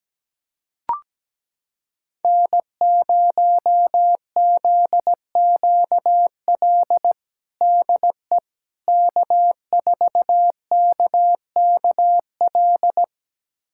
Morse code, heard as a keyed tone, about 17 wpm, 700 Hz. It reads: N0ZQL DE K4KKL
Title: USA Callsign DE Callsign - 5 Characters - 17wpm